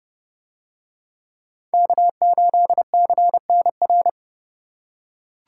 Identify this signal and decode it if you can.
Morse code, heard as a keyed tone, about 29 wpm, 700 Hz. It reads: K8CNR